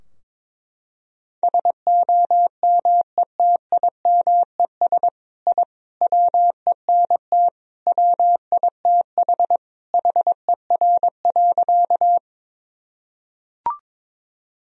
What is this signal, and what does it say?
Morse code, heard as a keyed tone, about 22 wpm, 700 Hz. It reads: SOMETIMES I WENT WITH HER.
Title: Sentences from Top 1000 Words - Review Part 3 - 22wpm